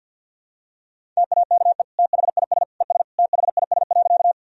Morse code, heard as a keyed tone, about 50 wpm, 700 Hz. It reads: TAKE THIS ES THIS.
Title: Sentences from Top 200 Words and Abbreviations 50wpm